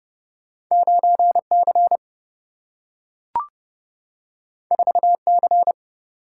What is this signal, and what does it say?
Morse code, heard as a keyed tone, about 30 wpm, 700 Hz. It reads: 9 C 4 C
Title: Single-Letter Single-Number 30wpm